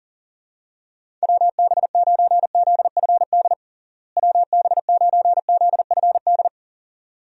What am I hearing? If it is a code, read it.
Morse code, heard as a keyed tone, about 40 wpm, 700 Hz. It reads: WB9ZFD WB9ZFD
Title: USA Call Signs - 6 Characters - Repeated 2x - 40wpm